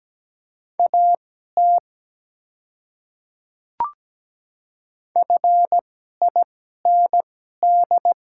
Morse code, heard as a keyed tone, about 17 wpm, spaced as for 12 wpm, 700 Hz. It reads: AT FIND
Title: 2x Character Spacing - Top 100 Words - 17wpm